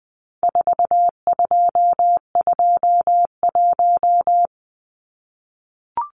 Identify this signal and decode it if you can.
Morse code, heard as a keyed tone, about 20 wpm, 700 Hz. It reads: 4221
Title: Numbers - 4-Digit 20wpm